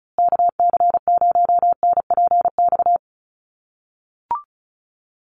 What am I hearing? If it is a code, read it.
Morse code, heard as a keyed tone, about 34 wpm, 700 Hz. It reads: KC0NPX
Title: USA Call Signs Repeated 2x 35wpm